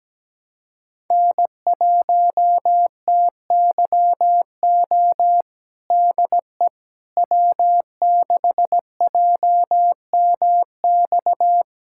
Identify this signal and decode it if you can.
Morse code, heard as a keyed tone, about 17 wpm, 700 Hz. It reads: N1TYO DE W6JMX
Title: USA Callsign DE Callsign - 5 Characters - 17wpm